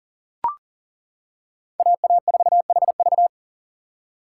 Morse code, heard as a keyed tone, about 37 wpm, 700 Hz. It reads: AA4HV
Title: K1USN SST QSO Practice - 40wpm